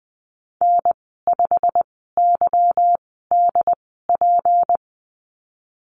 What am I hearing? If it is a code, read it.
Morse code, heard as a keyed tone, about 20 wpm, spaced as for 15 wpm, 700 Hz. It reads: N5YDP
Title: 186 Lesson 30g - Letter b - Call Signs 20wpm